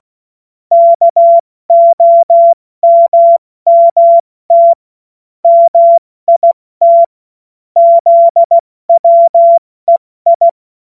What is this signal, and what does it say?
Morse code, heard as a keyed tone, about 16 wpm, 680 Hz. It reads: KOMMT MIT ZWEI
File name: Morsekurs / Witze / Witz 21. September 2021, 70 F80